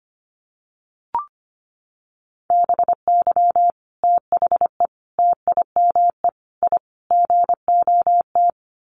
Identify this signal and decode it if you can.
Morse code, heard as a keyed tone, about 25 wpm, 700 Hz. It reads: BY THE TIME I GOT
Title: Sets of 5 Words Encore2 25wpm